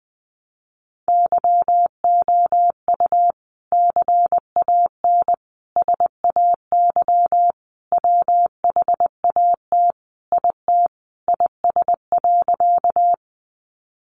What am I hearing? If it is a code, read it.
Morse code, heard as a keyed tone, about 19 wpm, 700 Hz. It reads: YOU CAN SAY WHAT IT IS.
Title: Sentences from Top 200 Words 20wpm